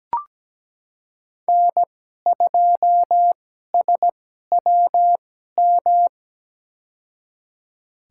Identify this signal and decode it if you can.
Morse code, heard as a keyed tone, about 17 wpm, spaced as for 12 wpm, 700 Hz. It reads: N2SWM
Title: USA Call Signs - 2x Character Spacing - 17wpm